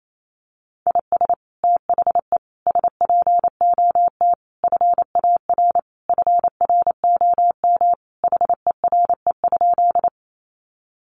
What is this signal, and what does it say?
Morse code, heard as a keyed tone, about 28 wpm, 700 Hz. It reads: IS THE SPOT FAR FROM HERE?
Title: Sentences from Top 1100 Words 28wpm